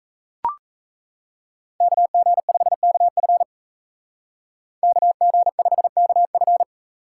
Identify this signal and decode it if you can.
Morse code, heard as a keyed tone, about 36 wpm, 700 Hz. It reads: KG5KF KG5KF